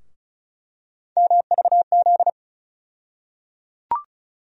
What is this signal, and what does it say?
Morse code, heard as a keyed tone, about 34 wpm, 700 Hz. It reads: MVZ
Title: USA Call Sign Postfixes - 3 Letters 35wpm